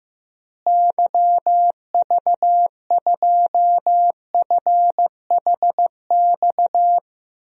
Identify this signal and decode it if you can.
Morse code, heard as a keyed tone, about 15 wpm, 700 Hz. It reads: YV2FHX
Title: DXCC-100 Call Signs - 6 Characters - Repeated 2x - 15wpm